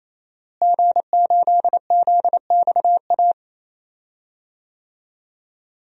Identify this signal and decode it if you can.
Morse code, heard as a keyed tone, about 28 wpm, 700 Hz. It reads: G8ZXA
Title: DXCC-100 Call Signs - 5 Characters - Repeated 3x - 28wpm